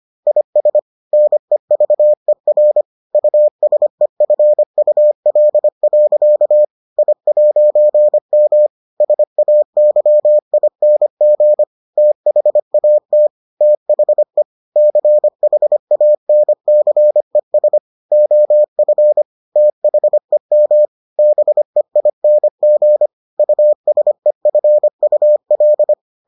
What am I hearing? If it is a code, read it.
Morse code, heard as a keyed tone, about 25 wpm, 600 Hz. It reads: IS NEVER USEFUL. I'M SAYING THAT THE CHANCES OF THEM BEING USEFUL